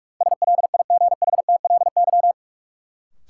Morse code, heard as a keyed tone, about 42 wpm, 700 Hz. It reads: SLIGHTLY
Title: Sentences from Top 1300 Words - New Word Drill - 45wpm